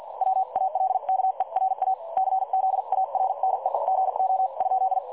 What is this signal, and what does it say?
Morse code, heard as a keyed tone, about 50 wpm, spaced as for 41 wpm, 760 Hz. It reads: WALKERA CZARNEGO?
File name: dlazaawansowanych / nagraniazpasma / websdr recording 3565.0kHz